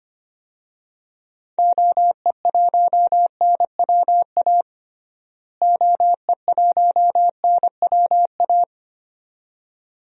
Morse code, heard as a keyed tone, about 25 wpm, 700 Hz. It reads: OE1NWA OE1NWA
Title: DXCC-100 Call Signs - 6 Characters - Repeated 2x - 25wpm